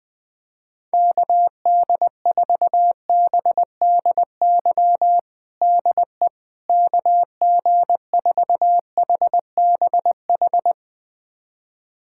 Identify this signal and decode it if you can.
Morse code, heard as a keyed tone, about 20 wpm, 700 Hz. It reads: KD4BDY DE KG4HBH
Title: USA Callsign DE Callsign 20wpm